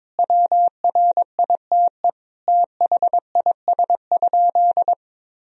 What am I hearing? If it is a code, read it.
Morse code, heard as a keyed tone, about 22 wpm, 700 Hz. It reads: WRITE THIS?
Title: Sentences from Top 1000 Words - Review Part 1 - 22wpm